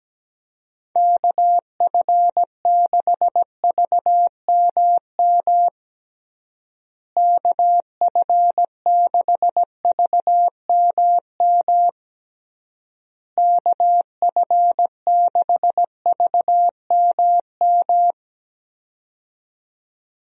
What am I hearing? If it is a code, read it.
Morse code, heard as a keyed tone, about 17 wpm, 700 Hz. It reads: KF6VMM KF6VMM KF6VMM
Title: USA Call Signs - 6 Characters - Repeated 3x - 17wpm